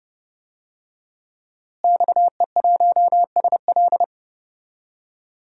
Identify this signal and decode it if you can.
Morse code, heard as a keyed tone, about 32 wpm, 700 Hz. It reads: XE1SL
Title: DXCC-100 Call Signs - 5 Characters - Repeated 3x - 30wpm